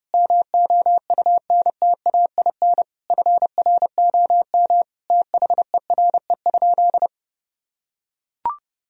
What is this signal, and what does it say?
Morse code, heard as a keyed tone, about 30 wpm, 700 Hz. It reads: MOUNTAIN FROM THERE?
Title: Sentences from Top 1000 Words - Review Part 1 - 30wpm